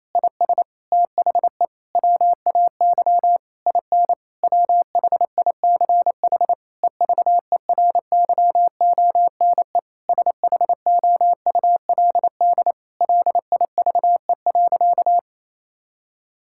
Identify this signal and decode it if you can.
Morse code, heard as a keyed tone, about 28 wpm, 700 Hz. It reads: IS THE WAY IN WHICH EVERYONE SHOULD LIVE.
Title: Sentences from Top 1000 Words - Review Part 5 - 28wpm